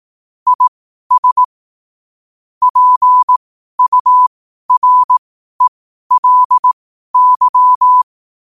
Morse code, heard as a keyed tone, about 18 wpm, spaced as for 12 wpm, 1000 Hz. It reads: IS PURELY
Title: 12 WPM morse code quotes for Fri, 01 Mar 2024